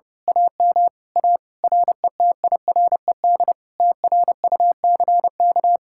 Morse code, heard as a keyed tone, about 28 wpm, 700 Hz. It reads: AM A RETIRED TRUCK
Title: Speed-Racing Common QSO Phrases Encore - 30wpm